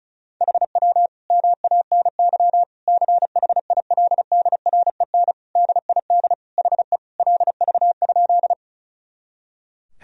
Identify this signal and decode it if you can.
Morse code, heard as a keyed tone, about 36 wpm, 700 Hz. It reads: HW MANY CHILDREN DID HE LV?
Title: Sentences from Top 400 Words and Abbreviations 35wpm